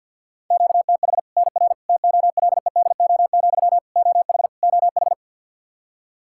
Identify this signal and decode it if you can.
Morse code, heard as a keyed tone, about 46 wpm, 700 Hz. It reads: QTH NR TOLEDO, OH OH
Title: Practice QSO - Exchange 1 - Beginning - 50wpm